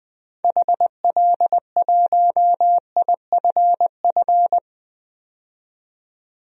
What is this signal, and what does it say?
Morse code, heard as a keyed tone, about 20 wpm, 700 Hz. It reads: HL1IFF
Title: DXCC-100 Call Signs - 6 Characters - 20wpm